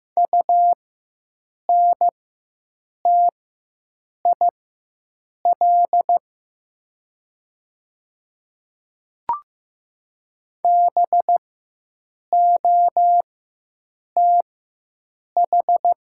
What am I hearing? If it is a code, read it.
Morse code, heard as a keyed tone, about 15 wpm, spaced as for 7 wpm, 700 Hz. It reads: UNTIL BOTH
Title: 4x Character Spacing - Top 300 Words - 15wpm